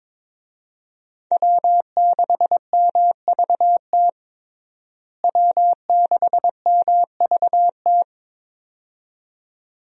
Morse code, heard as a keyed tone, about 22 wpm, 700 Hz. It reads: W6MVT W6MVT
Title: USA Call Signs - 5 Characters - Repeated 2x - 22wpm